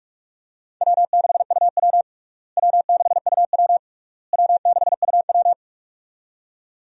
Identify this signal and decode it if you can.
Morse code, heard as a keyed tone, about 46 wpm, 700 Hz. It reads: W6UW W6UW W6UW
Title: USA Call Signs - 4 Characters - Repeated 3x - 45wpm